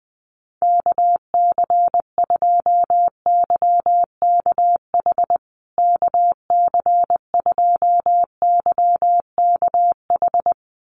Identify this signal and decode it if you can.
Morse code, heard as a keyed tone, about 20 wpm, 700 Hz. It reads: KC2YKH KC2YKH
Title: USA Call Signs Repeated 2x 20wpm